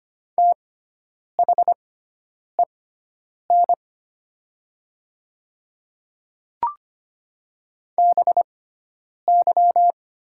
Morse code, heard as a keyed tone, about 25 wpm, spaced as for 9 wpm, 700 Hz. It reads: THEN BY
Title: 6x Character Spacing - Top 100 Words - 25wpm